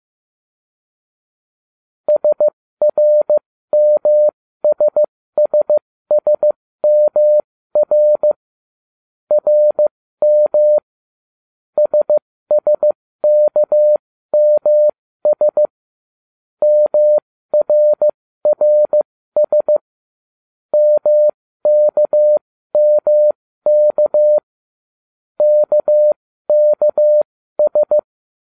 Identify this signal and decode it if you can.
Morse code, heard as a keyed tone, about 15 wpm, spaced as for 13 wpm, 600 Hz. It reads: SRMSSSMR RM SSKMS MRRS MKMK KKS